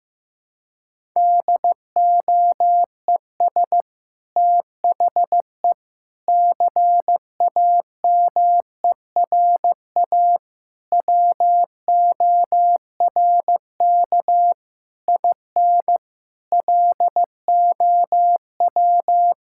Morse code, heard as a keyed tone, about 16 wpm, 700 Hz. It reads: DOES THE CAMERA WORK IN LOW
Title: Sentences from Top 1000 Words - Review Part 5 - 15wpm